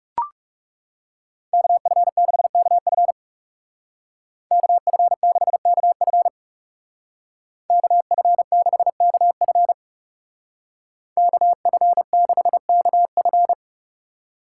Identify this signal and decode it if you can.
Morse code, heard as a keyed tone, about 36 wpm, 700 Hz. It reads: KF6KF KF6KF KF6KF KF6KF
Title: Speed Racing - USA 2x2 Call Signs 30wpm